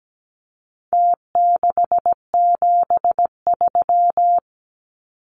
Morse code, heard as a keyed tone, about 17 wpm, 700 Hz. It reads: T673
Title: Numbers - Contest Style - 4-Digit 17wpm